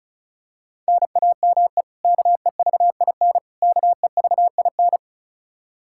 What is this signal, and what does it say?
Morse code, heard as a keyed tone, about 34 wpm, 700 Hz. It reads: NAME KEVIN KEVIN
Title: Practice QSO - Exchange 1 - Beginning - 35wpm